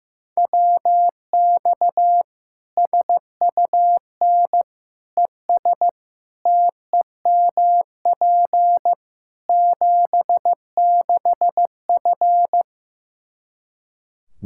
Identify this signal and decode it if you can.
Morse code, heard as a keyed tone, about 15 wpm, 700 Hz. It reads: WX SUN ES TEMP 76F
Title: K1USN SST QSO Practice 15wpm